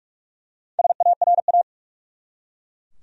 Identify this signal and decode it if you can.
Morse code, heard as a keyed tone, about 47 wpm, 700 Hz. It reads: SARA
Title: USA Common Names - Max 5 Letters - 45wpm